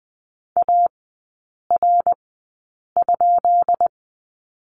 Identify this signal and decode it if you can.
Morse code, heard as a keyed tone, about 20 wpm, spaced as for 8 wpm, 700 Hz. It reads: AR?
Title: 133 Lesson 22e - Question Mark - Three Characters 20wpm